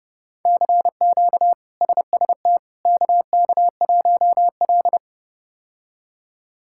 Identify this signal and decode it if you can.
Morse code, heard as a keyed tone, about 30 wpm, 700 Hz. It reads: CQ SST KK1L
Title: K1USN SST QSO Practice - 30wpm